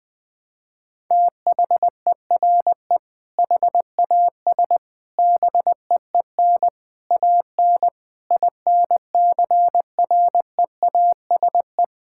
Morse code, heard as a keyed tone, about 20 wpm, 700 Hz. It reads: THERE HAS BEEN AN INCREASE
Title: Sentences with 6-Word Phrases - Spoken Beginning - 20wpm